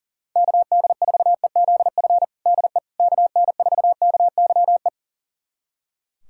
Morse code, heard as a keyed tone, about 40 wpm, 700 Hz. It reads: KD4EZF DE KN4KYE
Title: USA Callsign DE Callsign 40wpm